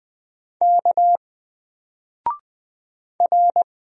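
Morse code, heard as a keyed tone, about 20 wpm, 700 Hz. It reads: K R
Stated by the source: Single letter and number Morse drill at 20 wpm, from N4LCD.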